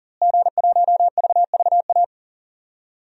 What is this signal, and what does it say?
Morse code, heard as a keyed tone, about 40 wpm, 700 Hz. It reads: G1VVA
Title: DXCC-100 Call Signs - 5 Characters - Repeated 2x - 40wpm